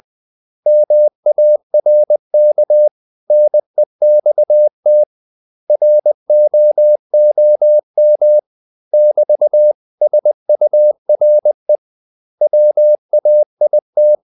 Morse code, heard as a keyed tone, about 20 wpm, 600 Hz. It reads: MARK NEXT ROOM <BT>SURE WAIT
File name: CW / Practice / 500MostCommonEnglishWords20